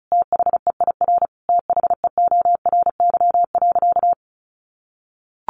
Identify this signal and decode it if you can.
Morse code, heard as a keyed tone, about 34 wpm, 700 Hz. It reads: THEIR THEORY.
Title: Sentences from Top 1400 Words 35wpm